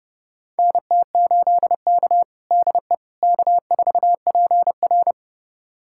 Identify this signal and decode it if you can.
Morse code, heard as a keyed tone, about 32 wpm, 700 Hz. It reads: NT8K DE K4PR
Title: USA Callsign DE Callsign - 4 Characters - 30wpm